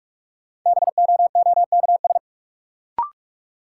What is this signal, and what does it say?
Morse code, heard as a keyed tone, about 46 wpm, 700 Hz. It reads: BOOKS
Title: Sentences from Top 1000 Words - Review Part 4 - New Word Drill - 45wpm